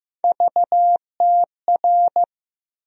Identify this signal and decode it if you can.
Morse code, heard as a keyed tone, about 15 wpm, 700 Hz. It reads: VTR
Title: USA Call Sign Postfixes - 3 Letters 15wpm